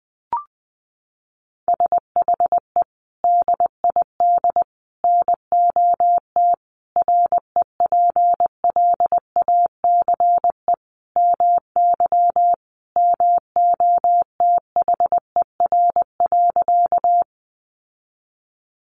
Morse code, heard as a keyed tone, about 20 wpm, 700 Hz. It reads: SHE DID NOT REPLACE MY MOTHER.
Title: Sentences from Top 1400 Words 20wpm